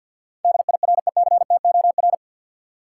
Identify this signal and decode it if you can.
Morse code, heard as a keyed tone, about 51 wpm, 700 Hz. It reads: DIRECTOR
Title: Sentences from Top 1300 Words - New Word Drill Alt - 50wpm